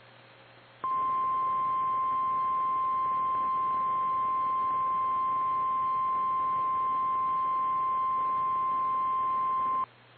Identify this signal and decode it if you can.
Emergency Alert System (SAME) tones (the 1050 Hz attention tone). It is heard from 0.8 to 9.9 s.
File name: eas / KKPRKXPN / KKPRKXPN